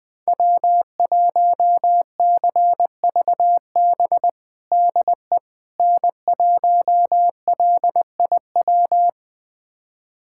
Morse code, heard as a keyed tone, about 21 wpm, 700 Hz. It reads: W1CVB DE N1LIW
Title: USA Callsign DE Callsign - 5 Characters - 20wpm